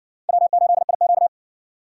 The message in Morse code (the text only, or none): W8IQ